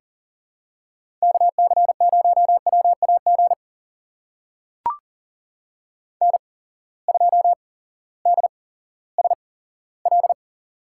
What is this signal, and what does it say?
Morse code, heard as a keyed tone, about 38 wpm, spaced as for 18 wpm, 700 Hz. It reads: KC0WAG N2DSL
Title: USA Call Signs - 8x Character Spacing - 40wpm